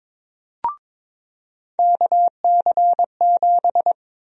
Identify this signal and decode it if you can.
Morse code, heard as a keyed tone, about 22 wpm, 700 Hz. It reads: KC7
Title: USA Call Sign Prefixes - 22wpm